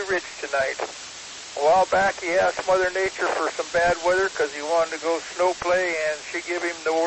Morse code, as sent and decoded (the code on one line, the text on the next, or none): none